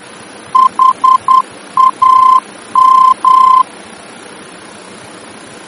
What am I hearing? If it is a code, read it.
Morse code, heard as a keyed tone, about 10 wpm, 1040 Hz. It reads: HAM
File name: germany / mx / HAM-D-VOR